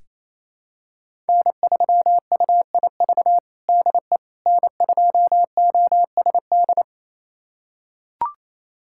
Morse code, heard as a keyed tone, about 27 wpm, 700 Hz. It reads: N3UIV DE N2OSD
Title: USA Callsign DE Callsign - 5 Characters - 28wpm